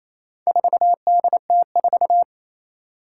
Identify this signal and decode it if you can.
Morse code, heard as a keyed tone, about 28 wpm, 700 Hz. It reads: 4DT4
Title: Four Challenging Letter-Number Combinations 28wpm